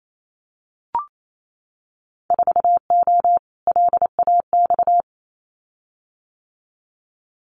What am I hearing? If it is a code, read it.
Morse code, heard as a keyed tone, about 28 wpm, 700 Hz. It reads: 4O LAX
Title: Winter Field Day Exchanges Encore 28wpm